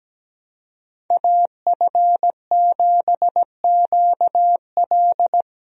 Morse code, heard as a keyed tone, about 17 wpm, 700 Hz. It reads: AF7QL
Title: USA Call Signs - 5 Characters - 17wpm